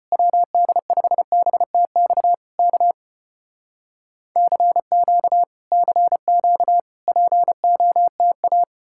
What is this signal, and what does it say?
Morse code, heard as a keyed tone, about 35 wpm, 700 Hz. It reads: WD5BTX K CQ CQ POTA